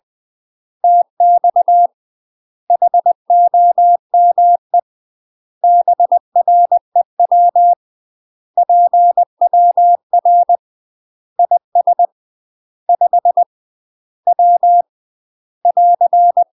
Morse code, heard as a keyed tone, about 20 wpm, 700 Hz. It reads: TX HOME BREW PWR IS 5 W <AR>
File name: UK MASTER QSOs TRAINING / 20WPM / UK MASTER QSOs 0045